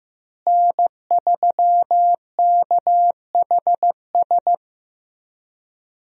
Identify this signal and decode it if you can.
Morse code, heard as a keyed tone, about 15 wpm, 700 Hz. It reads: N3KHS
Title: USA Call Signs - 5 Characters - Repeated 3x - 15wpm